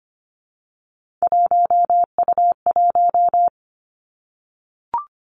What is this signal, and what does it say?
Morse code, heard as a keyed tone, about 25 wpm, 700 Hz. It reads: 1U1
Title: Three Challenging Letter-Number Combinations 25wpm